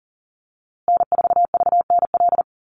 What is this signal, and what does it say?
Morse code, heard as a keyed tone, about 40 wpm, 700 Hz. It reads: N4VNL
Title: USA Call Signs Repeated 3x 40wpm